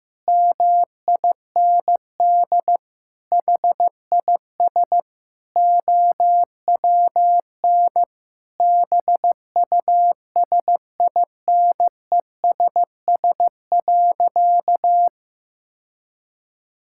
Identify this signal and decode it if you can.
Morse code, heard as a keyed tone, about 15 wpm, 700 Hz. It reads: MIND HIS OWN BUSINESS.
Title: Sentences from Top 1000 Words - Review Part 6 - 15wpm